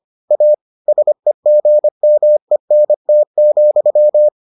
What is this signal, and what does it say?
Morse code, heard as a keyed tone, about 25 wpm, 600 Hz. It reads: A SEGMENT,